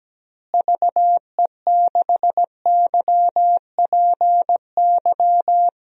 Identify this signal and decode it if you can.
Morse code, heard as a keyed tone, about 17 wpm, 700 Hz. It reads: VE6YPY
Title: DXCC-100 Call Signs - 6 Characters - Repeated 2x - 17wpm